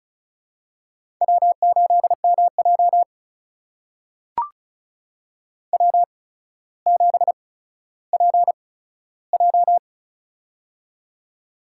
Morse code, heard as a keyed tone, about 36 wpm, spaced as for 15 wpm, 700 Hz. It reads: W8MJ W7PJ
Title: USA Call Signs - 4 Characters - 8x Character Spacing - 35wpm